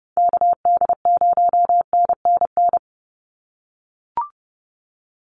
Morse code, heard as a keyed tone, about 30 wpm, 700 Hz. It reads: KD0NNN
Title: USA Call Signs 30wpm